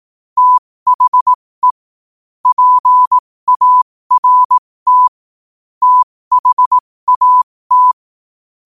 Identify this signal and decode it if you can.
Morse code, heard as a keyed tone, about 18 wpm, spaced as for 15 wpm, 1000 Hz. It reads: THE PART THAT
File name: qotdfiles / 15wpm